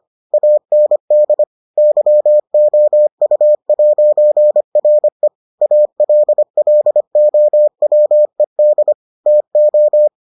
Morse code, heard as a keyed tone, about 25 wpm, 600 Hz. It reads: AND YOU'RE ALLOWED TO